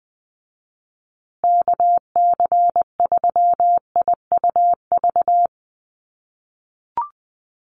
Morse code, heard as a keyed tone, about 19 wpm, 700 Hz. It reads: KC3IUV
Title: USA Call Signs Repeated 2x 20wpm